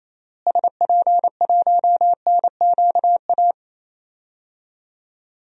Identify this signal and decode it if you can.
Morse code, heard as a keyed tone, about 28 wpm, 700 Hz. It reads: SP1NQA
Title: DXCC-100 Call Signs - 6 Characters - Repeated 3x - 28wpm